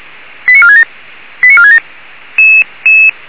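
Five-tone selective call (ZVEI), heard as quick runs of five tones, tones of about 80 ms. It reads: 89467 89467
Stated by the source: Five-tone ZVEI selective call 89391, received by SDR.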